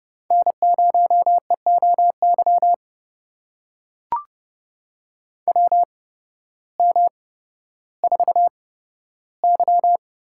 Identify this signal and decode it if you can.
Morse code, heard as a keyed tone, about 31 wpm, spaced as for 14 wpm, 700 Hz. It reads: N0EOY WM4Y
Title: USA Call Signs - 8x Character Spacing - 30wpm